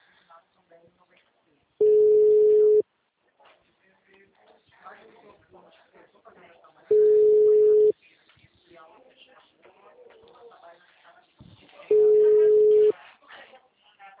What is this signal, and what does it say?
Ringback tone (Europe, 425 Hz), with speech.
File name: out- -2003- - - .56749